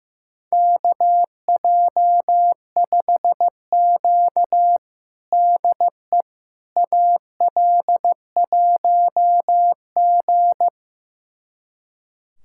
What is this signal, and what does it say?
Morse code, heard as a keyed tone, about 15 wpm, 700 Hz. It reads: KJ5Q DE AL1G
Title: USA Callsign DE Callsign - 4 Characters - 15wpm